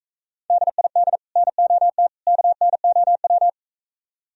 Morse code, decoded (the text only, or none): DID NOT KNOW